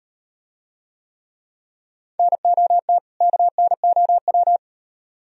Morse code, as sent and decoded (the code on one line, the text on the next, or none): -. --- - / -.- -. --- .--
NOT KNOW